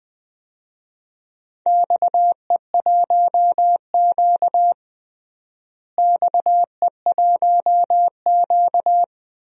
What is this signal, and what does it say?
Morse code, heard as a keyed tone, about 21 wpm, 700 Hz. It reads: XE1Q XE1Q
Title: DXCC-100 Call Signs - 4 Characters - Repeated 2x - 20wpm